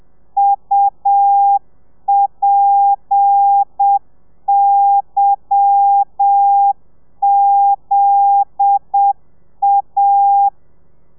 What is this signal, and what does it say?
Morse code, heard as a keyed tone, about 7 wpm, 800 Hz. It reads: UPYZA
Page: Lezione 1: audio a 30 CPM